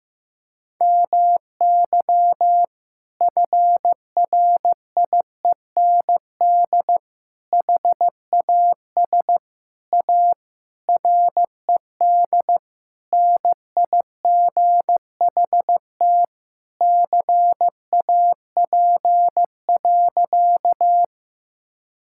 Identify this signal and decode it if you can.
Morse code, heard as a keyed tone, about 15 wpm, 700 Hz. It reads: MY FRIEND HAS A RED NIGHT CAP.
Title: Sentences from Top 1600 Words 15wpm